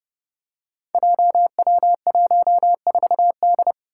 Morse code, heard as a keyed tone, about 30 wpm, 700 Hz. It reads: JW14D
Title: Five Challenging Letter-Number Combinations 30wpm